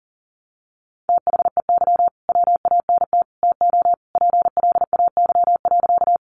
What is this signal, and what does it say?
Morse code, heard as a keyed tone, about 40 wpm, 700 Hz. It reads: THEY WANT TO PLAY.
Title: Easy Phrases 40wpm